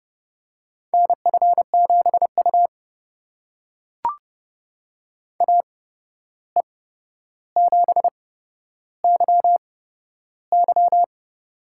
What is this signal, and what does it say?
Morse code, heard as a keyed tone, about 29 wpm, spaced as for 12 wpm, 700 Hz. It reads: NF7U AE7YY
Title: USA Call Signs - 8x Character Spacing - 30wpm